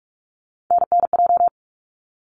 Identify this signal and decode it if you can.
Morse code, heard as a keyed tone, about 46 wpm, 700 Hz. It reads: NNJ
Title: ARRL Sections 45wpm